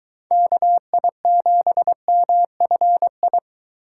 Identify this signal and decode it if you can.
Morse code, heard as a keyed tone, about 22 wpm, 700 Hz. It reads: KI7MFI